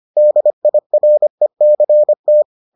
Morse code, heard as a keyed tone, about 25 wpm, 600 Hz. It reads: DIRECT